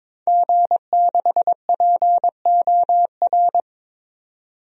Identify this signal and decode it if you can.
Morse code, heard as a keyed tone, about 22 wpm, 700 Hz. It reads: G6POR
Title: DXCC-100 Call Signs - 5 Characters - 22wpm